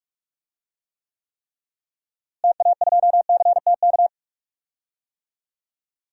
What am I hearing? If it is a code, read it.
Morse code, heard as a keyed tone, about 44 wpm, 700 Hz. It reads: TA2CTK
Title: DXCC-100 Call Signs - 6 Characters - Repeated 3x - 45wpm